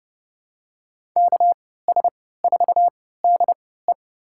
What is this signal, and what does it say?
Morse code, heard as a keyed tone, about 30 wpm, spaced as for 17 wpm, 700 Hz. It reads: KS4DE
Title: USA Call Signs - 5 Characters - 3x Character Spacing - 30wpm